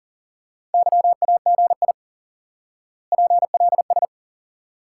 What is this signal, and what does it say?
Morse code, heard as a keyed tone, about 37 wpm, 700 Hz. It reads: YAGI PLS
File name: course / instant-qso-element-course-3 / IQE Course 3 - Lesson 89 - YAGI - Two Elements - 40wpm